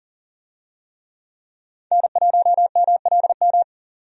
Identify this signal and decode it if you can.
Morse code, heard as a keyed tone, about 40 wpm, 700 Hz. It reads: N1MLM
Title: USA Call Signs - 5 Characters - 40wpm